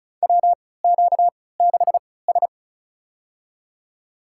Morse code, heard as a keyed tone, about 34 wpm, spaced as for 20 wpm, 700 Hz. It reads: WQ6S
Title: USA Call Signs - 3x Character Spacing - 35wpm